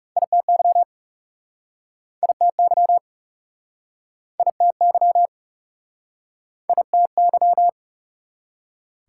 Morse code, heard as a keyed tone, about 42 wpm, 700 Hz. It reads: ITY ITY ITY ITY